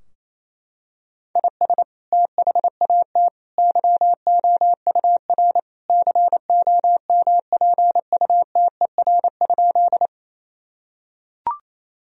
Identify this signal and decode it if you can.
Morse code, heard as a keyed tone, about 28 wpm, 700 Hz. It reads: IS THAT YOUR COMPUTER?
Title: Sentences from Top 1000 Words - Review Part 3 - 28wpm